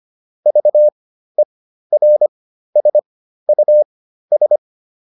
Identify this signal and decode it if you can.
Morse code, heard as a keyed tone, about 25 wpm, spaced as for 13 wpm, 600 Hz. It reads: VERSUS